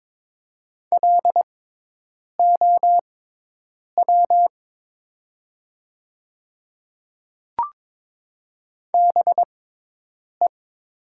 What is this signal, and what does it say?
Morse code, heard as a keyed tone, about 22 wpm, spaced as for 8 wpm, 700 Hz. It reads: LOW BE
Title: 6x Character Spacing - Top 300 Words - 22wpm